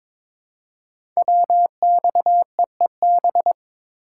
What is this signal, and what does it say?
Morse code, heard as a keyed tone, about 22 wpm, 700 Hz. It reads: WXEEB
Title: Code Groups - 5 Letters - 22wpm